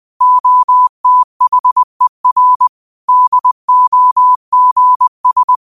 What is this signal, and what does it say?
Morse code, heard as a keyed tone, about 20 wpm, 1000 Hz. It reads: OTHER DOGS
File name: qotdfiles / 20wpm